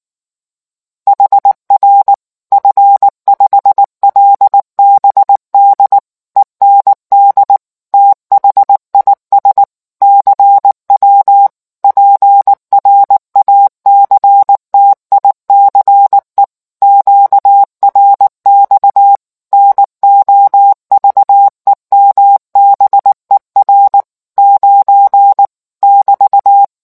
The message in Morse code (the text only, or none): HR F5LBD END THIS CW PRACTICE QRX NOVEMBER 9 <BT>